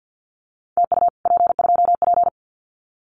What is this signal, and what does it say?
Morse code, heard as a keyed tone, about 50 wpm, 700 Hz. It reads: TU P2P
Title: Common QSO Phrases Encore 50wpm